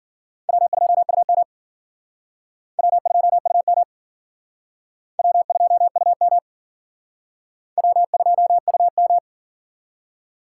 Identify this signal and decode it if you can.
Morse code, heard as a keyed tone, about 51 wpm, 700 Hz. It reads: W2UM W2UM W2UM W2UM